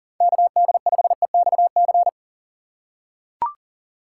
Morse code, heard as a keyed tone, about 38 wpm, 700 Hz. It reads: KD5EXC
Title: USA Call Signs - 3x Character Spacing - 40wpm